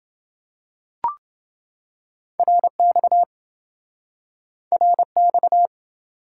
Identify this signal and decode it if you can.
Morse code, heard as a keyed tone, about 26 wpm, 700 Hz. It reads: RX RX